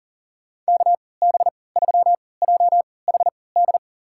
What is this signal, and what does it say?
Morse code, heard as a keyed tone, about 40 wpm, spaced as for 23 wpm, 700 Hz. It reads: KB3JHD
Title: USA Call Signs - 6 Characters - 3x Character Spacing - 40wpm